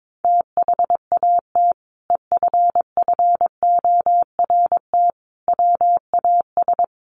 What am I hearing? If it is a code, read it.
Morse code, heard as a keyed tone, about 22 wpm, 700 Hz. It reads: THAT EFFORT WAS
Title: Sentences from Top 1200 Words 22wpm